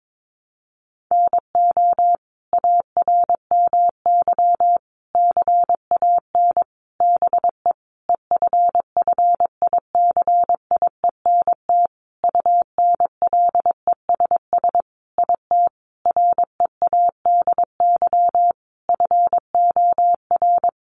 Morse code, heard as a keyed tone, about 22 wpm, 700 Hz. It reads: NO ARMY CAN BE EFFICIENT UNLESS IT READY FOR
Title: Sentences from Top 1300 Words 22wpm